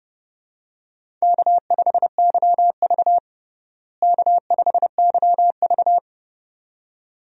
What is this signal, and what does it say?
Morse code, heard as a keyed tone, about 30 wpm, 700 Hz. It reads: K5YV K5YV
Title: USA Call Signs - 4 Characters - Repeated 2x - 30wpm